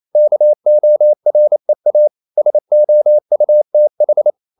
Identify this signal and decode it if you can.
Morse code, heard as a keyed tone, about 28 wpm, 600 Hz. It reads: KOREA SOUTH
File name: CW / Practice / NationalCapitals28